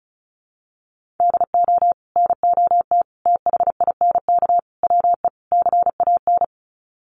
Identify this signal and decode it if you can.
Morse code, heard as a keyed tone, about 34 wpm, 700 Hz. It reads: DO NOT THINK WE CAN
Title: Sets of 5 Words Encore2 35wpm